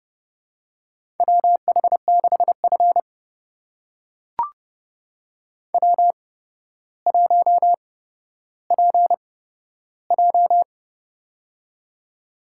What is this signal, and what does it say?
Morse code, heard as a keyed tone, about 30 wpm, spaced as for 13 wpm, 700 Hz. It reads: WH6F W1PJ
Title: USA Call Signs - 4 Characters - 8x Character Spacing - 30wpm